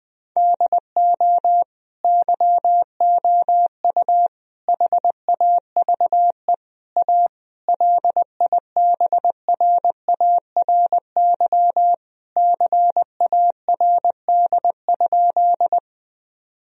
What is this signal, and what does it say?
Morse code, heard as a keyed tone, about 20 wpm, 700 Hz. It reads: DO YOU HAVE A LIBRARY CARD?
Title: Sentences from Top 1000 Words - Review Part 4 - 20wpm